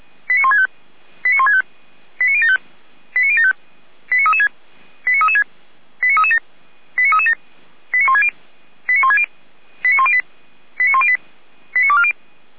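Five-tone selective call (ZVEI), heard as quick runs of five tones, tones of about 70 ms. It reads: 89165 89165 89075 89075 89336 89336 89337 89337 89160 89160 89108 89108 89240
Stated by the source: ZVEI five-tone selective call received on an SDR.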